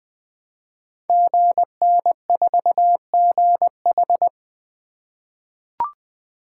Morse code, heard as a keyed tone, about 20 wpm, 700 Hz. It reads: GN4GH
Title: Code Groups - 5 Letter:Numbers - 20wpm